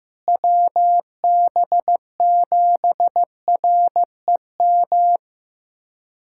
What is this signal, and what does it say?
Morse code, heard as a keyed tone, about 16 wpm, 700 Hz. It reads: WB7REM
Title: USA Callsign DE Callsign - 6 Characters - 15wpm